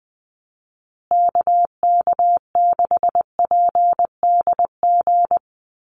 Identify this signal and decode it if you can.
Morse code, heard as a keyed tone, about 20 wpm, 700 Hz. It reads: KK6PDG
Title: USA Call Signs Repeated 3x 20wpm